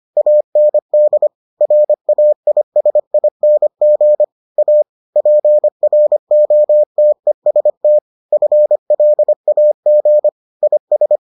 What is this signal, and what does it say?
Morse code, heard as a keyed tone, about 25 wpm, 600 Hz. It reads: AND RAISING A PROTEST FLAG IS